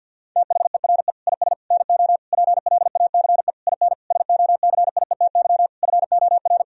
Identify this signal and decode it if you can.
Morse code, heard as a keyed tone, about 50 wpm, 700 Hz. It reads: THERE IS NO PLACE IN SOCIETY FOR
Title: Sentences from Top 1000 Words - Review Part 4 - 50wpm